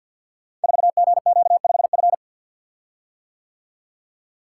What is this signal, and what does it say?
Morse code, heard as a keyed tone, about 48 wpm, 690 Hz. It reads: 4GQ5F
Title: Code Groups - 5 Letter:Numbers - 50wpm